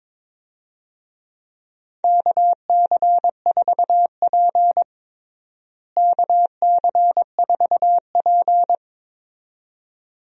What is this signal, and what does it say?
Morse code, heard as a keyed tone, about 22 wpm, 700 Hz. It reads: KC4P KC4P
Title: USA Call Signs - 4 Characters - Repeated 2x - 22wpm